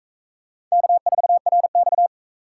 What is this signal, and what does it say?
Morse code, heard as a keyed tone, about 42 wpm, 700 Hz. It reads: K4RX